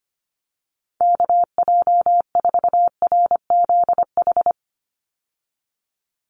Morse code, heard as a keyed tone, about 25 wpm, 700 Hz. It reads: KJ4RZH
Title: USA Call Signs 25wpm